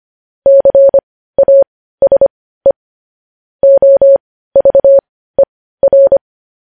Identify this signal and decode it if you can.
Morse code, heard as a keyed tone, about 25 wpm, spaced as for 15 wpm, 550 Hz. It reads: CASE OVER